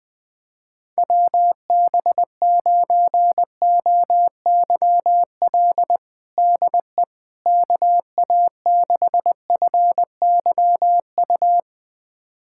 Morse code, heard as a keyed tone, about 20 wpm, 700 Hz. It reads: WB9OYL DE KA6FYU
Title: USA Callsign DE Callsign - 6 Characters - 20wpm